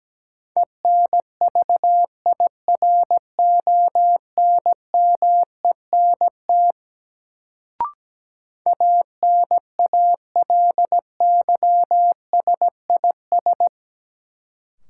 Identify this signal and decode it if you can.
Morse code, heard as a keyed tone, about 17 wpm, 700 Hz. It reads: ENVIRONMENT ANALYSIS
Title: Sentences from Top 1000 Words - Review Part 5 - New Word Drill - 17wpm